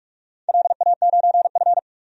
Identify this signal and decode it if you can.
Morse code, heard as a keyed tone, about 44 wpm, 700 Hz. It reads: LA9F
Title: DXCC-100 Call Signs - 4 Characters - 45wpm